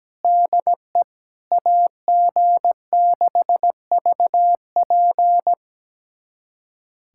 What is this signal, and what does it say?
Morse code, heard as a keyed tone, about 17 wpm, 700 Hz. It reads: DE AG6VP
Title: USA Callsign DE Callsign - 5 Characters - 17wpm